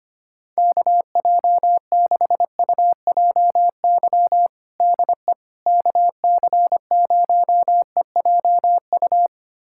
Morse code, heard as a keyed tone, about 25 wpm, 700 Hz. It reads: KJ6UJY DE KC0EJU
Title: USA Callsign DE Callsign - 6 Characters - 25wpm